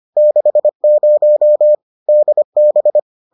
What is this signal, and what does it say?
Morse code, heard as a keyed tone, about 25 wpm, 600 Hz. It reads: 60 DB